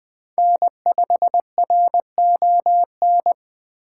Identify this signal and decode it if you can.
Morse code, heard as a keyed tone, about 20 wpm, 700 Hz. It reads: N5RON